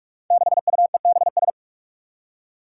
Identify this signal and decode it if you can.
Morse code, heard as a keyed tone, about 46 wpm, 700 Hz. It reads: 6UEBS
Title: Five Challenging Letter-Number Combinations 45wpm